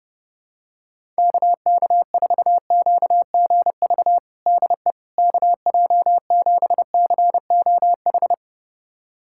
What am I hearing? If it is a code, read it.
Morse code, heard as a keyed tone, about 30 wpm, 700 Hz. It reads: KK4QGV DE KJ7COH
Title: USA Callsign DE Callsign 30wpm